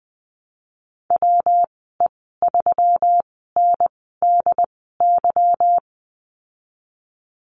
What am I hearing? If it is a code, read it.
Morse code, heard as a keyed tone, about 20 wpm, spaced as for 15 wpm, 700 Hz. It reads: WE3NDY
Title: 154 Lesson 25g - Letter p - Call Signs 20wpm